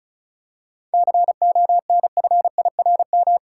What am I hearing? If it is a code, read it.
Morse code, heard as a keyed tone, about 34 wpm, 700 Hz. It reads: CONFIRM
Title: Seven Letter Words - 35wpm